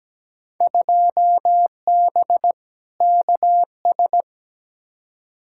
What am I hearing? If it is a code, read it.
Morse code, heard as a keyed tone, about 17 wpm, 700 Hz. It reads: 2B KS
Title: Field Day Exchanges Encore - 17wpm